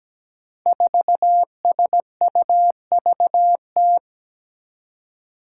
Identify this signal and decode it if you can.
Morse code, heard as a keyed tone, about 17 wpm, 700 Hz. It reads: 4SUVT
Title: Five Challenging Letter-Number Combinations 17wpm